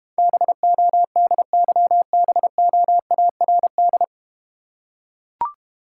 Morse code, heard as a keyed tone, about 32 wpm, 700 Hz. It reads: BODYBOARD